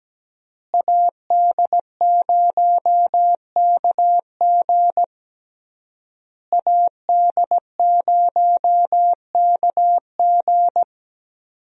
Morse code, heard as a keyed tone, about 17 wpm, 700 Hz. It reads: AD0KG AD0KG
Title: USA Call Signs - 5 Characters - Repeated 3x - 17wpm